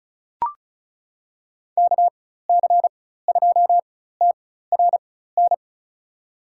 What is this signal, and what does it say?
Morse code, heard as a keyed tone, about 36 wpm, spaced as for 16 wpm, 700 Hz. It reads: KC2TRN
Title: USA Call Signs - 6 Characters - 4x Character Spacing - 35wpm